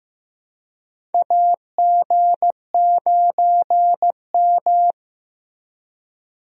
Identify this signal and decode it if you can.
Morse code, heard as a keyed tone, about 15 wpm, 700 Hz. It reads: AG9M
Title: USA Call Signs - 4 Characters - Repeated 2x - 15wpm